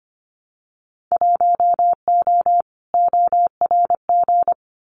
Morse code, heard as a keyed tone, about 25 wpm, 700 Hz. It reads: 1O ORG
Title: Winter Field Day Exchanges 25wpm